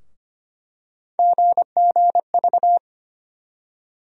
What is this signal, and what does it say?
Morse code, heard as a keyed tone, about 25 wpm, 700 Hz. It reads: GGV
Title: USA Call Sign Postfixes - 3 Letters 25wpm